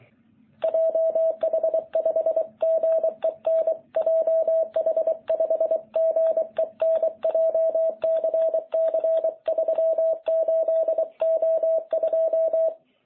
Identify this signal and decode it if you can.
Morse code, heard as a keyed tone, about 23 wpm, 660 Hz. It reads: JH5GEN JH5GEN JCC 38O2